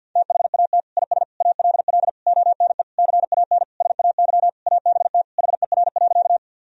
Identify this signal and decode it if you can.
Morse code, heard as a keyed tone, about 49 wpm, 700 Hz. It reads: THAT IS ALL ONE CAN SAY ABT HER.
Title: Sentences from Top 200 Words and Abbreviations 50wpm